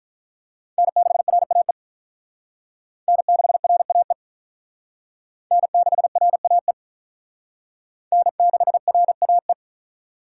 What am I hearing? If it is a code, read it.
Morse code, heard as a keyed tone, about 44 wpm, 700 Hz. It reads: N6RAE N6RAE N6RAE N6RAE
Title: Speed Racing - USA 1x3 Call Signs 35wpm